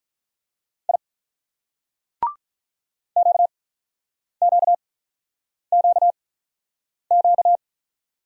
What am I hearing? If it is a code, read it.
Morse code, heard as a keyed tone, about 44 wpm, spaced as for 9 wpm, 700 Hz. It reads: I QQQQ